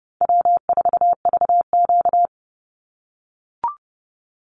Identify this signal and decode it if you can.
Morse code, heard as a keyed tone, about 30 wpm, 700 Hz. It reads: W4VQ
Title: Easy Call Signs - 30wpm